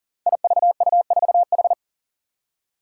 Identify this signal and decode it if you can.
Morse code, heard as a keyed tone, about 40 wpm, 700 Hz. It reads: IVU4H
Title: Five Challenging Letter-Number Combinations 40wpm